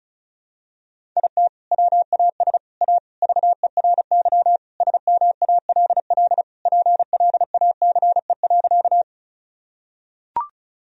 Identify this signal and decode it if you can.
Morse code, heard as a keyed tone, about 36 wpm, 700 Hz. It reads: IT WAS A VERY SMALL PLACE.
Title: Sentences from Top 1000 Words - Review Part 1 - 35wpm